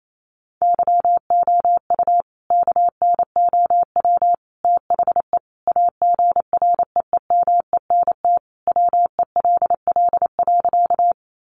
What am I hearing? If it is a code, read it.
Morse code, heard as a keyed tone, about 28 wpm, 700 Hz. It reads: YOU KNOW THE AGREEMENT WELL.
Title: Sentences from Top 1500 Words 28wpm